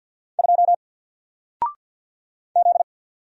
Morse code, heard as a keyed tone, about 50 wpm, 700 Hz. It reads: 2 Z
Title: Single Letter-Number 50wpm